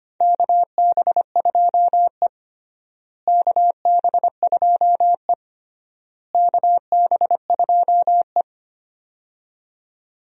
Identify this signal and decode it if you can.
Morse code, heard as a keyed tone, about 25 wpm, 700 Hz. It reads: KB2E KB2E KB2E
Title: USA Call Signs - 4 Characters - Repeated 3x - 25wpm